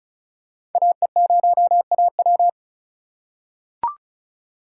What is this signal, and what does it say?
Morse code, heard as a keyed tone, about 36 wpm, 700 Hz. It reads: AE0AW
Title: USA Call Signs - 5 Characters - 4x Character Spacing - 35wpm